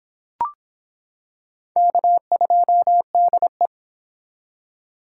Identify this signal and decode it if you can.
Morse code, heard as a keyed tone, about 25 wpm, 700 Hz. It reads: K2DE